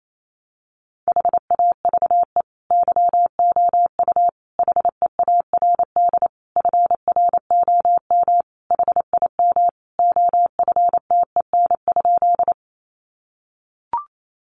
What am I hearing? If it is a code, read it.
Morse code, heard as a keyed tone, about 28 wpm, 700 Hz. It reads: HAVE YOU HEARD FROM HIM OFTEN?
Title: Sentences from Top 400 Words 28wpm